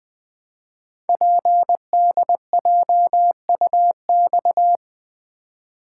Morse code, heard as a keyed tone, about 20 wpm, 700 Hz. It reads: PDJUX
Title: Code Groups - 5 Letters - 20wpm